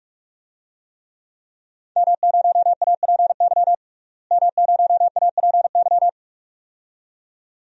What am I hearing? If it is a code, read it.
Morse code, heard as a keyed tone, about 41 wpm, 700 Hz. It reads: M0APY M0APY
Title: DXCC-100 Call Signs - 5 Characters - Repeated 2x - 45wpm